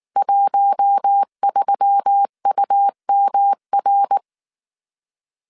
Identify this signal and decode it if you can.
Morse code, heard as a keyed tone, about 19 wpm, 800 Hz. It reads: 13UMR